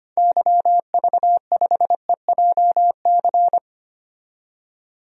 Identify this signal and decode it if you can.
Morse code, heard as a keyed tone, about 25 wpm, 700 Hz. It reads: YV5EJC
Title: DXCC-100 Call Signs - 6 Characters - 25wpm